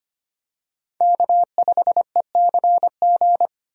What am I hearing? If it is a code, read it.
Morse code, heard as a keyed tone, about 25 wpm, 700 Hz. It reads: K5ECG
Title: USA Call Signs - 5 Characters - 25wpm